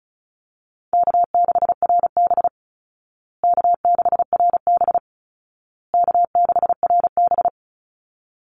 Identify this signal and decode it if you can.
Morse code, heard as a keyed tone, about 34 wpm, 700 Hz. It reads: K6RB K6RB K6RB
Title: Easy Call Signs Repeated 3x - 35wpm